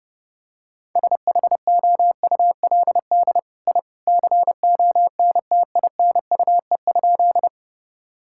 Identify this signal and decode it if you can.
Morse code, heard as a keyed tone, about 30 wpm, 700 Hz. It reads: SHOULD I CONTINUE?
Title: Sentences from Top 1000 Words - Review Part 6 - 30wpm